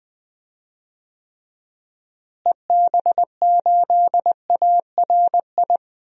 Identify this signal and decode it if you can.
Morse code, heard as a keyed tone, about 20 wpm, 700 Hz. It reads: EB8ARI
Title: DXCC-100 Call Signs - 6 Characters - Repeated 3x - 20wpm